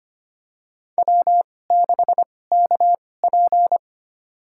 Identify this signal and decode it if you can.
Morse code, heard as a keyed tone, about 25 wpm, spaced as for 18 wpm, 700 Hz. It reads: W6KP
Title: USA Call Signs - 4 Characters - 2x Character Spacing - 25wpm